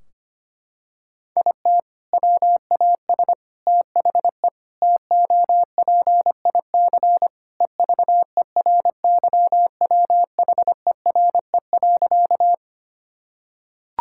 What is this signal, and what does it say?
Morse code, heard as a keyed tone, about 25 wpm, 700 Hz. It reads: IT WAS THE TOPIC EVERYWHERE.
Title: Sentences from Top 1600 Words 25wpm